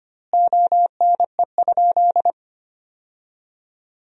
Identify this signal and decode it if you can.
Morse code, heard as a keyed tone, about 25 wpm, 700 Hz. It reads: ONE?
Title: 2x Word-Spacing - Sentences from Top 100 Words - 25wpm